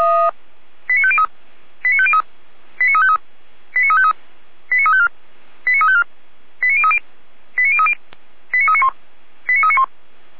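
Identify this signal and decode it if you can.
Five-tone selective call (ZVEI), heard as quick runs of five tones, tones of about 70 ms. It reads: 89503 89503 89353 89353 89354 89354 89030 89030 89391 89391
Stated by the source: ZVEI five-tone call received by SDR, logged by Feuerwehr Orlamünde (fire brigade).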